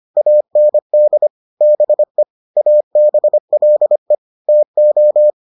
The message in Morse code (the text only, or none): AND BE ABLE TO